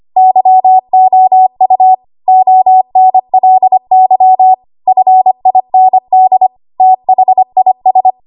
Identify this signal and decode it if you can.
Morse code, heard as a keyed tone, about 25 wpm, 750 Hz. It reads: YOU ONLY FIND THIS
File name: Archive / 25 WPM / 25WPM